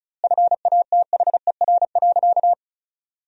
Morse code, heard as a keyed tone, about 34 wpm, 700 Hz. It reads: FATHER.
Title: 2x Word-Spacing - Sentences from Top 500 Words - 35wpm